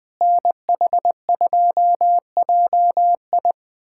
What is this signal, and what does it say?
Morse code, heard as a keyed tone, about 20 wpm, 700 Hz. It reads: NH2JI
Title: USA Callsign DE Callsign - 5 Characters - 20wpm